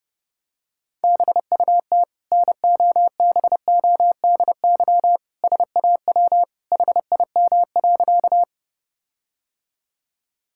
Morse code, heard as a keyed tone, about 30 wpm, 700 Hz. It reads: BUT NOBODY SAW HIM.
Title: Sentences from Top 1600 Words 30wpm